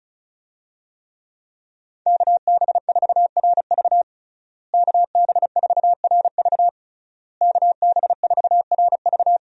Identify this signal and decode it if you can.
Morse code, heard as a keyed tone, about 34 wpm, 700 Hz. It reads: KB4RV KB4RV KB4RV
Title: USA Call Signs - 5 Characters - Repeated 3x - 35wpm